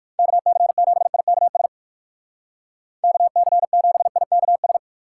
Morse code, heard as a keyed tone, about 57 wpm, 700 Hz. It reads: KC7IKS KC7IKS